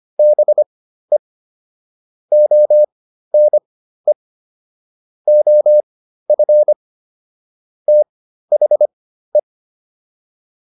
Morse code, heard as a keyed tone, about 25 wpm, spaced as for 13 wpm, 600 Hz. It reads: BE ONE OF THE
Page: /intermediate-practice-files/